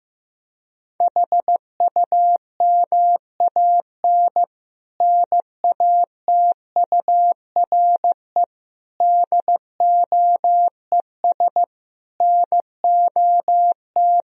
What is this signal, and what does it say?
Morse code, heard as a keyed tone, about 15 wpm, 700 Hz. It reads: HUMAN NATURE DOES NOT
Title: Sentences from Top 1000 Words - Review Part 3 - 15wpm